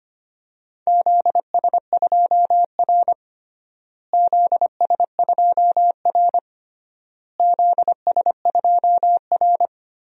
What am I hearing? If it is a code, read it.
Morse code, heard as a keyed tone, about 25 wpm, 700 Hz. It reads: ZS2R ZS2R ZS2R
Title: DXCC-100 Call Signs - 4 Characters - Repeated 3x - 25wpm